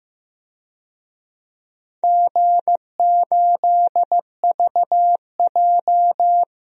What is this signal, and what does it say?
Morse code, heard as a keyed tone, about 15 wpm, 700 Hz. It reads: G8VJ
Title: DXCC-100 Call Signs - 4 Characters - Repeated 3x - 15wpm